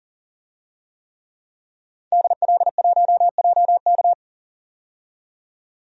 Morse code, heard as a keyed tone, about 40 wpm, 700 Hz. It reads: DL1JK
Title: DXCC-100 Call Signs - 5 Characters - 40wpm